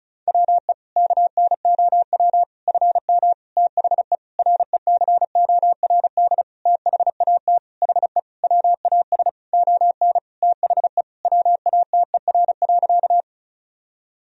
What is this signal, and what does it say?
Morse code, heard as a keyed tone, about 36 wpm, 700 Hz. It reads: WE KNOW FM THE RECORD THAT HE WAS ON THE WATER.
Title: Sentences from Top 500 Words and Abbreviations 35wpm